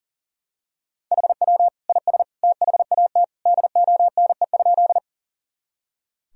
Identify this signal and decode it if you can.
Morse code, heard as a keyed tone, about 40 wpm, 700 Hz. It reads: HW IS THAT DONE?
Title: Sentences from Top 500 Words and Abbreviations 40wpm